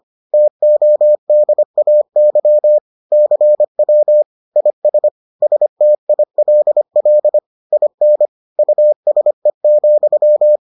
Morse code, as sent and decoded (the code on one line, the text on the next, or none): - --- -.. .- -.-- / -.-. .-- / .. ... / ... - .. .-.. .-.. / .. -. / ..- ... --..--
TODAY CW IS STILL IN USE,